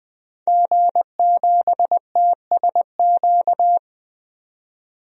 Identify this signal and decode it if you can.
Morse code, heard as a keyed tone, about 20 wpm, 700 Hz. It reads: G7TSQ
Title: DXCC-100 Call Signs - 5 Characters - 20wpm